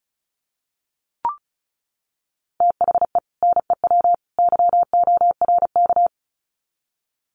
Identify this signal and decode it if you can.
Morse code, heard as a keyed tone, about 34 wpm, 700 Hz. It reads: THE NEW YORK
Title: Sets of 3 Words 35wpm